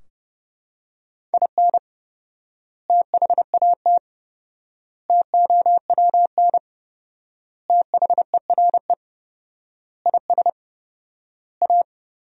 Morse code, heard as a keyed tone, about 32 wpm, 700 Hz. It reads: IN THAT TOWN THERE IS A
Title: 4x Word-Spacing - Sentences from Top 500 Words - 30wpm